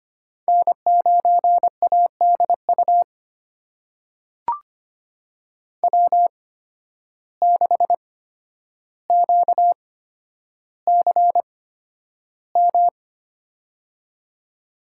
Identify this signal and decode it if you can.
Morse code, heard as a keyed tone, about 25 wpm, spaced as for 11 wpm, 700 Hz. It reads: N9ADU W6QCM
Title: USA Call Signs - 5 Characters - 8x Character Spacing - 25wpm